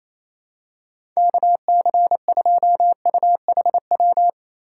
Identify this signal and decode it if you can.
Morse code, heard as a keyed tone, about 28 wpm, 700 Hz. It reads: KC2UHW
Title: USA Call Signs - 4x Character Spacing - 28wpm